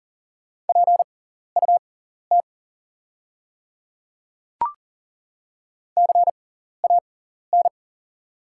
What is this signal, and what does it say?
Morse code, heard as a keyed tone, about 40 wpm, spaced as for 14 wpm, 700 Hz. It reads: PUT CAN
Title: 6x Character Spacing - Top 200 Words - 40wpm